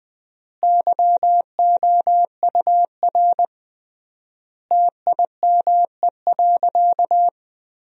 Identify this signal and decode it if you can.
Morse code, heard as a keyed tone, about 20 wpm, 700 Hz. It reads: YOUR TIME.
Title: 3x Word-Spacing - Sentences from Top 100 Words - 20wpm